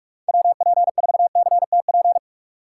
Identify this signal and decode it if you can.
Morse code, heard as a keyed tone, about 45 wpm, 700 Hz. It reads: WP4CTP
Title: USA Call Signs - 6 Characters - 4x Character Spacing - 45wpm